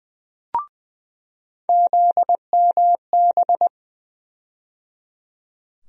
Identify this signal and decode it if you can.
Morse code, heard as a keyed tone, about 20 wpm, 700 Hz. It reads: ZMB
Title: USA Call Sign Postfixes - 3 Letters 20wpm